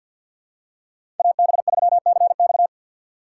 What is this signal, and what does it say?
Morse code, heard as a keyed tone, about 54 wpm, 700 Hz. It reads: AB3CX
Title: USA Call Signs - 5 Characters - 8x Character Spacing - 50wpm